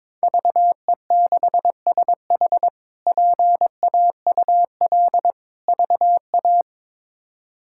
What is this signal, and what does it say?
Morse code, heard as a keyed tone, about 21 wpm, 700 Hz. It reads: VE6SH PAUL VA
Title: K1USN SST QSO Practice - 22wpm